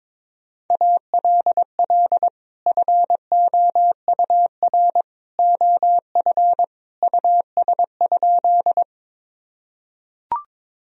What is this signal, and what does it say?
Morse code, heard as a keyed tone, about 22 wpm, 700 Hz. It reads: ALL FOUR OF US?
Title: Sentences from Top 1000 Words - Review Part 1 - 22wpm